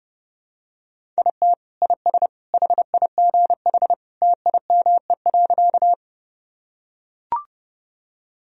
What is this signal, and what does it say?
Morse code, heard as a keyed tone, about 29 wpm, 700 Hz. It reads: IT IS HIGH TIME.
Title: Sentences from Top 1000 Words - Review Part 1 - 30wpm